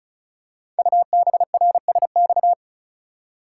Morse code, heard as a keyed tone, about 36 wpm, 700 Hz. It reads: UBRSX
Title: Code Groups - 5 Letters - 35wpm